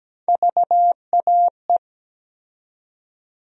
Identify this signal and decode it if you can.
Morse code, heard as a keyed tone, about 17 wpm, 700 Hz. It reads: VAE
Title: USA Call Sign Postfixes - 3 Letters 17wpm